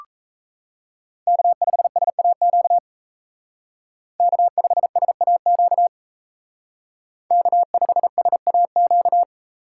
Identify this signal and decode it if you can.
Morse code, heard as a keyed tone, about 35 wpm, 700 Hz. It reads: K5SAQ K5SAQ K5SAQ